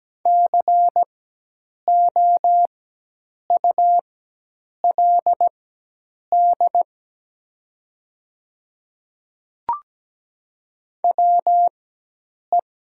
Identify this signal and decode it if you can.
Morse code, heard as a keyed tone, about 17 wpm, spaced as for 8 wpm, 690 Hz. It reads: COULD WE
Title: 4x Character Spacing - Top 200 Words - 17wpm